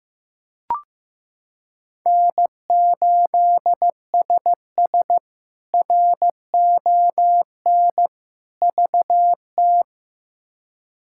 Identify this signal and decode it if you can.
Morse code, heard as a keyed tone, about 15 wpm, 700 Hz. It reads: N8SS RON VT
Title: K1USN SST QSO Practice - 15wpm